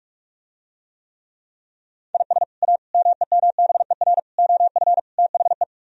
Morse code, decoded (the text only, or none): IS A MEMBER OF THE